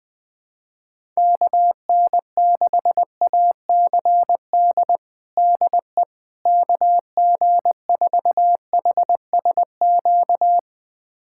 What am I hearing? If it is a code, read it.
Morse code, heard as a keyed tone, about 20 wpm, 700 Hz. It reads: KN6ACD DE KG4HSQ
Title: USA Callsign DE Callsign 20wpm